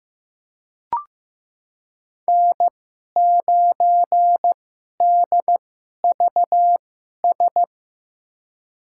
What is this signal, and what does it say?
Morse code, heard as a keyed tone, about 15 wpm, spaced as for 11 wpm, 700 Hz. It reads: N9DVS
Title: USA Call Signs - 5 Characters - 2x Character Spacing - 15wpm